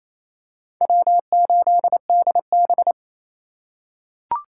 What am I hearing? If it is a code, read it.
Morse code, heard as a keyed tone, about 27 wpm, 700 Hz. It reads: W8DB
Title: USA Call Signs - 4 Characters - 4x Character Spacing - 28wpm